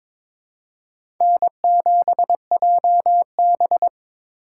Morse code, heard as a keyed tone, about 22 wpm, 700 Hz. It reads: N7JB
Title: USA Call Signs - 4 Characters - 6x Character Spacing - 22wpm